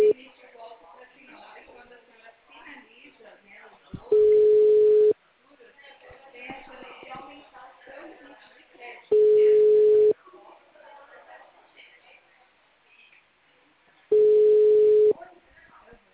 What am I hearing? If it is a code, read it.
Ringback tone (Europe, 425 Hz), with speech.